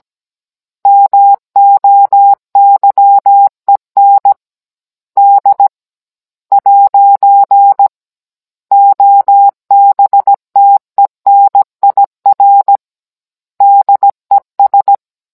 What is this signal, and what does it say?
Morse code, heard as a keyed tone, about 17 wpm, spaced as for 14 wpm, 800 Hz. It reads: MOYEN D'OBTENIR DES